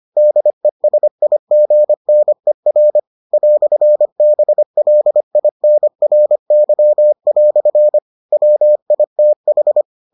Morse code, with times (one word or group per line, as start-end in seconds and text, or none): 0.2-3.0 DESIGNER
3.3-8.0 "BLINRY"
8.3-9.8 WITH